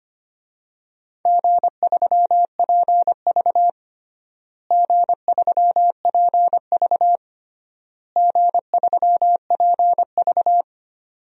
Morse code, heard as a keyed tone, about 25 wpm, 700 Hz. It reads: G3PV G3PV G3PV
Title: DXCC-100 Call Signs - 4 Characters - Repeated 3x - 25wpm